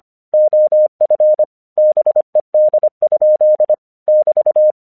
Morse code, heard as a keyed tone, about 25 wpm, 610 Hz. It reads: OF BED? <BT>
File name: 1 English 25wpm / AQC34